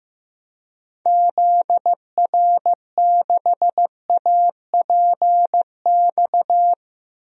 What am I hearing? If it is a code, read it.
Morse code, heard as a keyed tone, about 16 wpm, 700 Hz. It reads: ZR6APX
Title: DXCC-100 Call Signs - 6 Characters - Repeated 2x - 15wpm